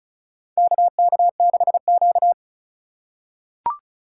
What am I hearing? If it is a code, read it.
Morse code, heard as a keyed tone, about 34 wpm, 700 Hz. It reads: KK6Q
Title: USA Call Signs - 4x Character Spacing - 35wpm